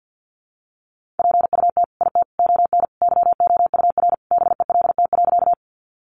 Morse code, heard as a keyed tone, about 50 wpm, 700 Hz. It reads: PUT IT ON YOUR BELT.
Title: Sentences from Top 1400 Words 50wpm